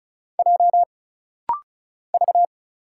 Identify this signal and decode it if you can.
Morse code, heard as a keyed tone, about 36 wpm, 700 Hz. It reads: J V
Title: Single Letter-Number - Rapid-Fire 35wpm